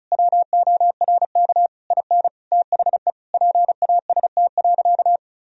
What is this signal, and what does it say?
Morse code, heard as a keyed tone, about 36 wpm, 700 Hz. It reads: WORK IN THE PAST.
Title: Sentences from Top 1000 Words - Review Part 4 - 35wpm